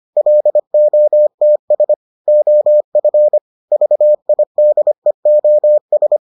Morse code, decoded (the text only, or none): LOTS OF VIDEOS